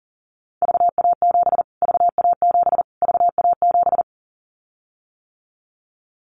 Morse code, heard as a keyed tone, about 40 wpm, 700 Hz. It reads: VA7 VA7 VA7
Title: Callsign Prefixes for Top 25 DXCC - Repeated 3x - 40wpm